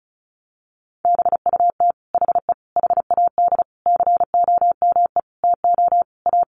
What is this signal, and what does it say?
Morse code, heard as a keyed tone, about 34 wpm, 700 Hz. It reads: BUT HE HAD COME TO A
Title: Sentences from Top 1500 Words 35wpm